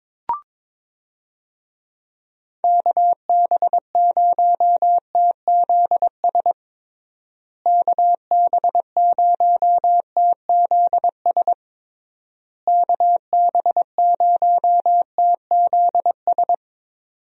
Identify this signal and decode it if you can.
Morse code, heard as a keyed tone, about 22 wpm, 700 Hz. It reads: KB0TZS KB0TZS KB0TZS
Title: USA Call Signs - 6 Characters - Repeated 3x - 22wpm